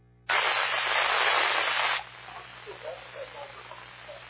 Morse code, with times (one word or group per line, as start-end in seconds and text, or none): none